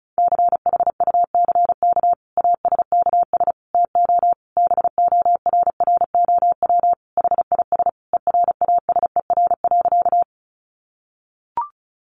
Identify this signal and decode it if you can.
Morse code, heard as a keyed tone, about 34 wpm, 700 Hz. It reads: CHUCK ASKS TO BORROW HIS ERASER.
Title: Additional Easy Phrases 35wpm